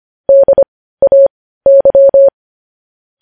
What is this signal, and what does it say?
Morse code, heard as a keyed tone, about 25 wpm, spaced as for 15 wpm, 550 Hz. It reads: DAY
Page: news headlines in morse code at 15 wpm